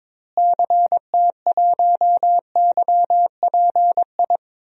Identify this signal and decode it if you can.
Morse code, heard as a keyed tone, about 21 wpm, 700 Hz. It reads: CT1YPI
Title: DXCC-100 Call Signs - 6 Characters - Repeated 3x - 22wpm